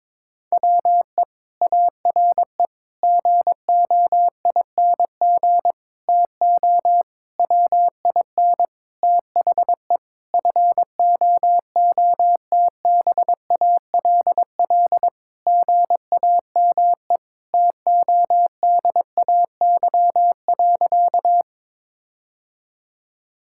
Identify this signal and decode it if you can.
Morse code, heard as a keyed tone, about 22 wpm, 700 Hz. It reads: WE ARE GOING TO WIN THE FOOTBALL GAME TODAY.
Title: Sentences from Top 1600 Words 22wpm